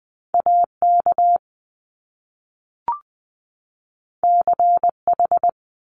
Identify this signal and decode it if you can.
Morse code, heard as a keyed tone, about 20 wpm, 700 Hz. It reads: AK CH
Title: Top Two Letter Combinations in Words 20wpm